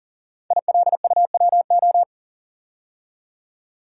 Morse code, heard as a keyed tone, about 40 wpm, 700 Hz. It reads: ILUWO
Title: Code Groups - 5 Letters - 40wpm